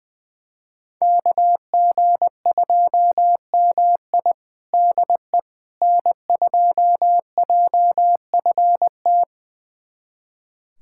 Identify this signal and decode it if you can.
Morse code, heard as a keyed tone, about 20 wpm, 700 Hz. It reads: KG2MI DE N2JFT
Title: USA Callsign DE Callsign 20wpm